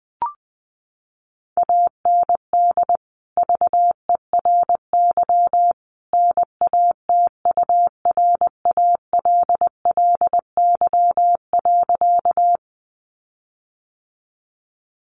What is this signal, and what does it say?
Morse code, heard as a keyed tone, about 20 wpm, 700 Hz. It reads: AND VERY NATURALLY.
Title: Sentences from Top 1300 Words 20wpm